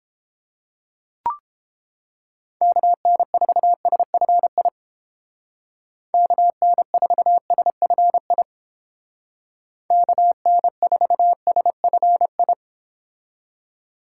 Morse code, heard as a keyed tone, about 29 wpm, 700 Hz. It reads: KN4SFI KN4SFI KN4SFI